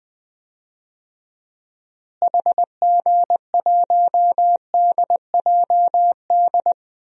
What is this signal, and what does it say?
Morse code, heard as a keyed tone, about 20 wpm, 700 Hz. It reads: HG1DJD
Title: DXCC-100 Call Signs - 6 Characters - Repeated 3x - 20wpm